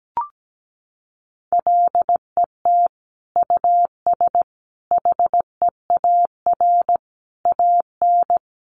Morse code, heard as a keyed tone, about 17 wpm, 700 Hz. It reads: LET US HEAR AN
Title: Sentences from Top 900 Words 17wpm